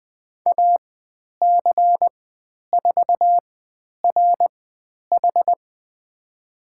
Morse code, heard as a keyed tone, about 20 wpm, spaced as for 10 wpm, 700 Hz. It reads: AC4RH
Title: 102 Lesson 17g - Letter u - Call Signs 20wpm